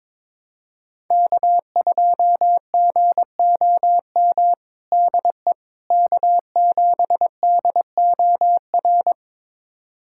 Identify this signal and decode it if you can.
Morse code, heard as a keyed tone, about 21 wpm, 700 Hz. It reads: K2GOM DE K7DOR
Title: USA Callsign DE Callsign - 5 Characters - 22wpm